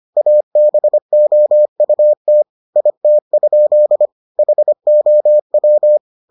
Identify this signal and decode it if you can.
Morse code, heard as a keyed tone, about 25 wpm, 600 Hz. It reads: ABOUT IT? HOW